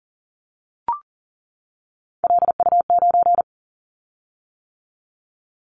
Morse code, heard as a keyed tone, about 41 wpm, 700 Hz. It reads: LU9
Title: Callsign Prefixes for Top 50 DXCC - 40wpm